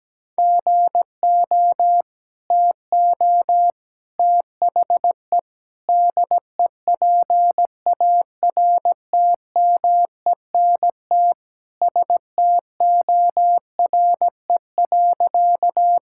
Morse code, heard as a keyed tone, about 17 wpm, 700 Hz. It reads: GO TO THE DEPARTMENT STORE.
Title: Sentences from Top 1000 Words - Review Part 5 - 17wpm